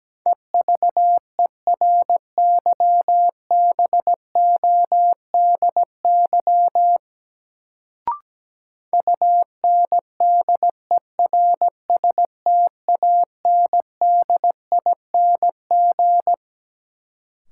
Morse code, heard as a keyed tone, about 17 wpm, 700 Hz. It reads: EVERYBODY UNDERSTANDING
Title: Sentences from Top 1400 Words - New Word Drill Alt - 17wpm